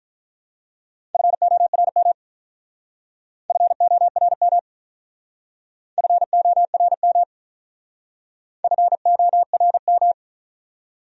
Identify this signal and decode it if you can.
Morse code, heard as a keyed tone, about 45 wpm, 700 Hz. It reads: FORM FORM FORM FORM